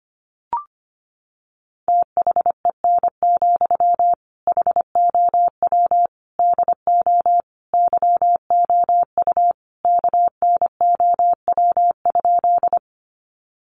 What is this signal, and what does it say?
Morse code, heard as a keyed tone, about 25 wpm, 700 Hz. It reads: THEN, HOW DO YOU KNOW?
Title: Sentences from Top 100 Words 25wpm